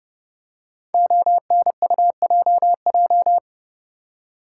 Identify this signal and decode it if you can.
Morse code, heard as a keyed tone, about 29 wpm, 700 Hz. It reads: ONUJJ
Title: Code Groups - 5 Letters - 30wpm